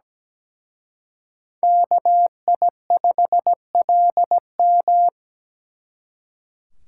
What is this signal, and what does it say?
Morse code, heard as a keyed tone, about 20 wpm, 700 Hz. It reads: KI5LM
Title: Speed Racing - USA 2x2 Call Signs 17wpm